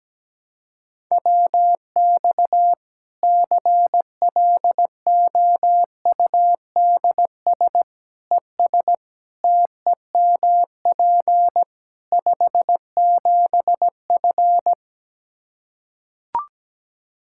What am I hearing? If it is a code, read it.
Morse code, heard as a keyed tone, about 17 wpm, 700 Hz. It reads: WX CLOUDS ES TEMP 57F
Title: Practice QSO - Exchange 2 - Info - 17wpm